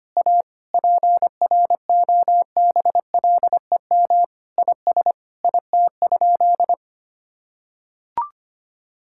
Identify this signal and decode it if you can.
Morse code, heard as a keyed tone, about 25 wpm, 700 Hz. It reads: A PROBLEM IS IT?
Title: Sentences from Top 1000 Words - Review Part 6 - 25wpm